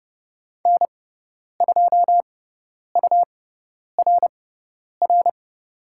Morse code, heard as a keyed tone, about 30 wpm, spaced as for 10 wpm, 700 Hz. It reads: N2URR
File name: Lesson 18g - Letter m - Call Signs 30wpm